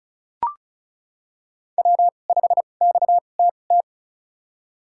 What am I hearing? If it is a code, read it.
Morse code, heard as a keyed tone, about 34 wpm, spaced as for 25 wpm, 700 Hz. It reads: W5XTT
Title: USA Call Signs - 5 Characters - 2x Character Spacing - 35wpm